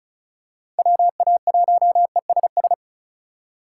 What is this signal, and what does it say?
Morse code, heard as a keyed tone, about 34 wpm, 700 Hz. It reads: WA1ESS
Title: USA Call Signs - 3x Character Spacing - 35wpm